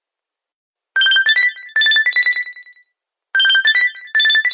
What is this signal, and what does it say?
There is no code.